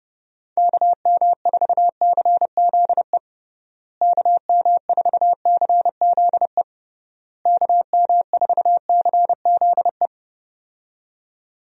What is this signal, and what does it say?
Morse code, heard as a keyed tone, about 30 wpm, 700 Hz. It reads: KM4CZE KM4CZE KM4CZE
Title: USA Call Signs - 6 Characters - Repeated 3x - 30wpm